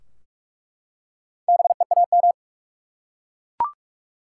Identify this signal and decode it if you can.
Morse code, heard as a keyed tone, about 46 wpm, 700 Hz. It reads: BEAM
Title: Top 90 QSO Elements 45wpm